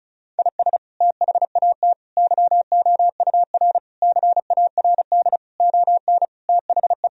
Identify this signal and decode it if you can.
Morse code, heard as a keyed tone, about 34 wpm, 700 Hz. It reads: IS THAT YOUR CARD ON THE
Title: Sentences from Top 1000 Words - Review Part 4 - 35wpm